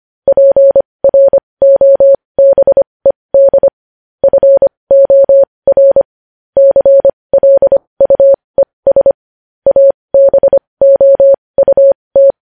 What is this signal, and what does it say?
Morse code, heard as a keyed tone, about 25 wpm, spaced as for 20 wpm, 550 Hz. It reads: PROBED FOR CLUES ABOUT
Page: News Headlines in Morse Code at 20 WPM